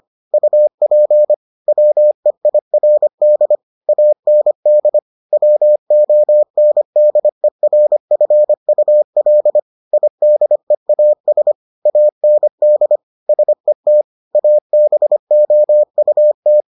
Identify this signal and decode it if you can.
Morse code, heard as a keyed tone, about 25 wpm, 600 Hz. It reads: UP WEIRD AND WONDERFUL IDEAS AND SET ABOUT